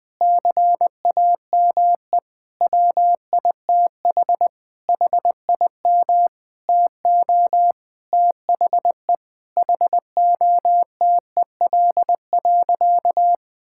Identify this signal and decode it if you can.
Morse code, heard as a keyed tone, about 20 wpm, 700 Hz. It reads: CAME WITH HIM TO THE HOTEL.
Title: Sentences from Top 1600 Words 20wpm